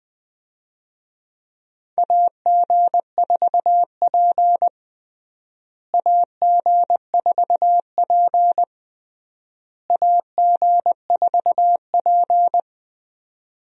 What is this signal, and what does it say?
Morse code, heard as a keyed tone, about 20 wpm, 700 Hz. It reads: AG4P AG4P AG4P
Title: USA Call Signs - 4 Characters - Repeated 3x - 20wpm